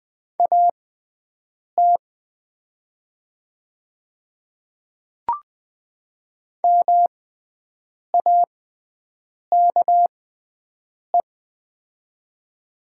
Morse code, heard as a keyed tone, about 20 wpm, spaced as for 7 wpm, 690 Hz. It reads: AT MAKE
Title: 6x Character Spacing - Top 100 Words - 20wpm